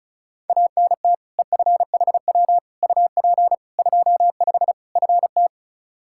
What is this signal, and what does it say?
Morse code, heard as a keyed tone, about 34 wpm, 700 Hz. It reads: ANT EFHW UP 25 FT
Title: Practice QSO - Exchange 2 - Info - 35wpm